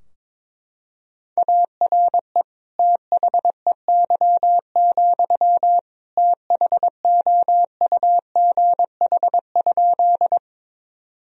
Morse code, heard as a keyed tone, about 22 wpm, 700 Hz. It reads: ARE THEY, THOUGH?
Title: Sentences from Top 1000 Words - Review Part 2 - 22wpm